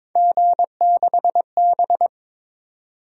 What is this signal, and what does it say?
Morse code, heard as a keyed tone, about 22 wpm, 700 Hz. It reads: G6B
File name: practice / 22wpm / Three Letter-Number Combinations - 22wpm